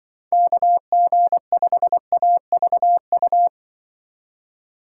Morse code, heard as a keyed tone, about 25 wpm, 700 Hz. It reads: KG5AVU
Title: Speed Racing - USA 2x3 Call Signs 20wpm